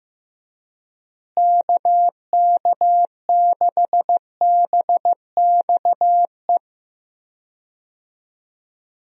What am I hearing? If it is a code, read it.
Morse code, heard as a keyed tone, about 15 wpm, 700 Hz. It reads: KK6BXE
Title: USA Call Signs - 6 Characters - Repeated 3x - 15wpm